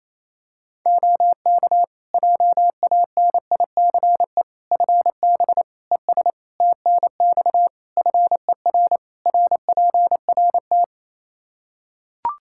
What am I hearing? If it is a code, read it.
Morse code, heard as a keyed tone, about 28 wpm, 700 Hz. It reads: OK JANICE FB ES TNX FER RPRT
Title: K1USN SST QSO Practice 28wpm